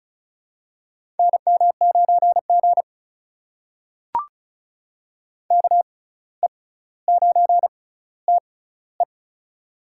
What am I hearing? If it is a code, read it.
Morse code, heard as a keyed tone, about 33 wpm, spaced as for 17 wpm, 700 Hz. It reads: NM9G KE9TE
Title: USA Call Signs - 6x Character Spacing - 35wpm